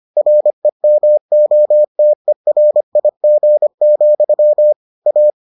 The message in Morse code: REMOTERIG, A

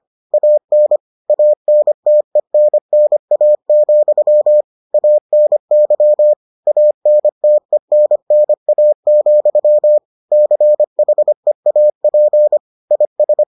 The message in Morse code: AN ANTENNA, ANY ANTENNA, CHEAP IS

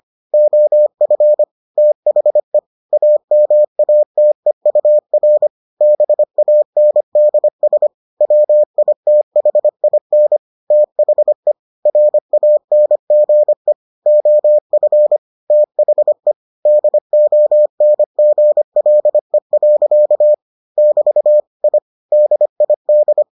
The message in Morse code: OF THE AMATEUR BANDS WITHIN THE RANGE OF THE DONGLE. <BT> I DID

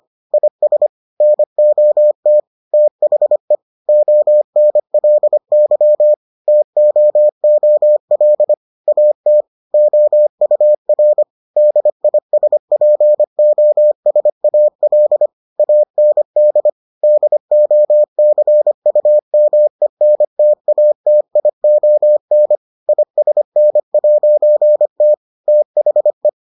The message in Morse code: IS NOT THE ONLY TOOL AT OUR DISPOSAL AND DOCUMENTATION ISN'T THE

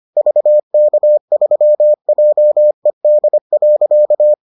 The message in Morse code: VK3JED.

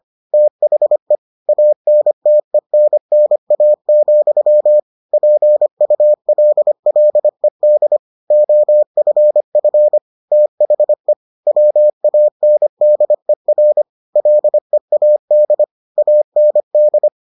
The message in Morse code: THE ANTENNA, PULLED OFF THE WANDER LEAD AND

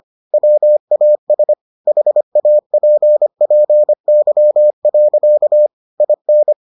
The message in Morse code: WAS HAPPY. IN

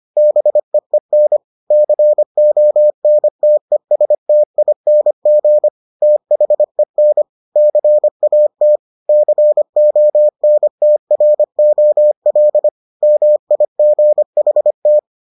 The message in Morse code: BEEN CONTESTING THEN CAT CONTROL MIGHT

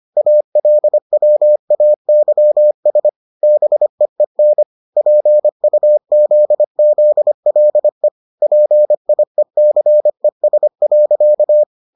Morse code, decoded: ALWAYS BEEN PUZZLE PIECES.

